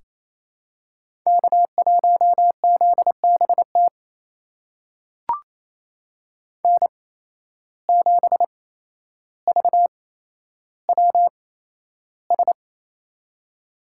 Morse code, heard as a keyed tone, about 28 wpm, spaced as for 12 wpm, 700 Hz. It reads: K1ZBT N7VWS
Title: USA Call Signs - 5 Characters - 8x Character Spacing - 28wpm